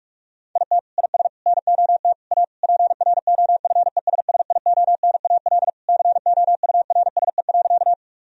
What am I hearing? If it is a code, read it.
Morse code, heard as a keyed tone, about 46 wpm, 700 Hz. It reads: IT IS NOT A PROFESSIONAL COURSE.
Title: Sentences from Top 1000 Words - Review Part 4 - 45wpm